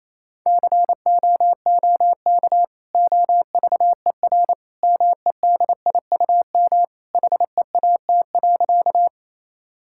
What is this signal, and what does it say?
Morse code, heard as a keyed tone, about 27 wpm, 700 Hz. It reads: COOK OVER MEDIUM HEAT.
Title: Sentences from Top 1000 Words - Review Part 6 - 28wpm